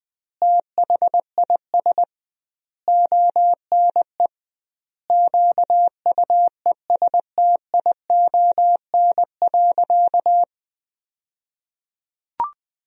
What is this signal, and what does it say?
Morse code, heard as a keyed tone, about 20 wpm, 700 Hz. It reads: THIS ONE QUESTION.
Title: 2x Word-Spacing - Sentences from Top 400 Words - 20wpm